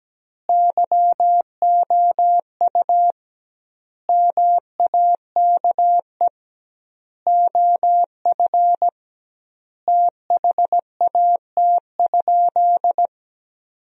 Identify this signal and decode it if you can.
Morse code, heard as a keyed tone, about 17 wpm, 700 Hz. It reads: YOU MAKE OF THAT?
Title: 2x Word-Spacing - Sentences from Top 100 Words - 17wpm